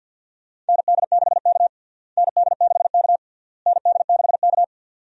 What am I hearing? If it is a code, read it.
Morse code, heard as a keyed tone, about 47 wpm, 700 Hz. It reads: ND6K ND6K ND6K
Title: USA Call Signs - 4 Characters - Repeated 3x - 50wpm